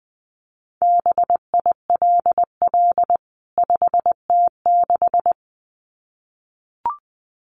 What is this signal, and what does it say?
Morse code, heard as a keyed tone, about 20 wpm, 700 Hz. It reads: BILL 5T6
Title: CWT Exchange 20wpm